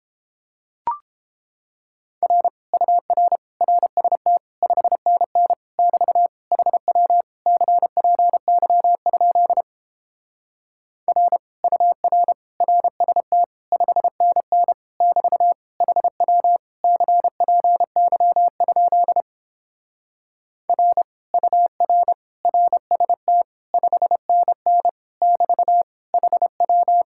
Morse code, heard as a keyed tone, about 29 wpm, 700 Hz. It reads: R UR RST 5NN <BT> HW CPY? R UR RST 5NN <BT> HW CPY? R UR RST 5NN <BT> HW